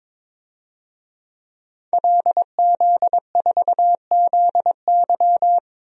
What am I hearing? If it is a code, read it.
Morse code, heard as a keyed tone, about 22 wpm, 700 Hz. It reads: LZ4ZY
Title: DXCC-100 Call Signs - 5 Characters - 22wpm